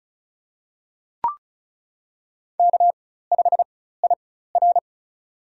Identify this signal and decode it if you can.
Morse code, heard as a keyed tone, about 34 wpm, spaced as for 16 wpm, 700 Hz. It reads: K5IR
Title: USA Call Signs - 4 Characters - 4x Character Spacing - 35wpm